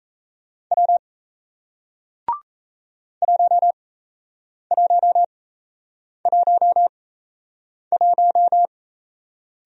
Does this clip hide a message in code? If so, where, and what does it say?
Yes: Morse code from 0.7 to 8.7 s, heard as a keyed tone, about 37 wpm, spaced as for 8 wpm, 700 Hz. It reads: W 1111